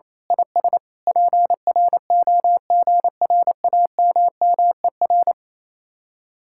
Morse code, heard as a keyed tone, about 26 wpm, 700 Hz. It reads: IS PROGRAMMER
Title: Speed-Racing Common QSO Phrases - 28wpm